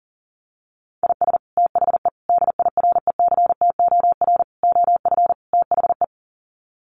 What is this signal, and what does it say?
Morse code, heard as a keyed tone, about 40 wpm, 700 Hz. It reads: IS THE DIRECTOR OF THE
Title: Sets of 5 Words Encore2 40wpm